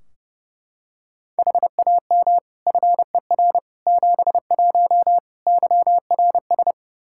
Morse code, heard as a keyed tone, about 30 wpm, 700 Hz. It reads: HAM FER 71 YRS
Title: K1USN SST QSO Practice 30wpm